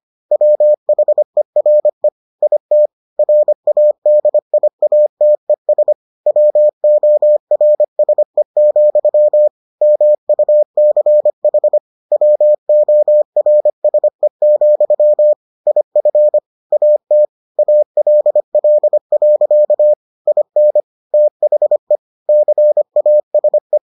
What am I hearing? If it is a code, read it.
Morse code, heard as a keyed tone, about 25 wpm, 600 Hz. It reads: WHERE IT RADIATES WORSE, MUCH WORSE, IF AT ALL. IN THE CASE